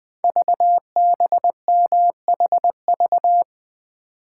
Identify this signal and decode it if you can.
Morse code, heard as a keyed tone, about 20 wpm, 700 Hz. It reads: VBMHV
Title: Code Groups - 5 Letters - 20wpm